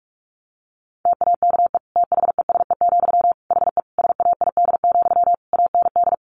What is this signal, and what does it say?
Morse code, heard as a keyed tone, about 44 wpm, 700 Hz. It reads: TAKE THESE, HE SAID, AND
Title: Sentences from Top 300 Words 45wpm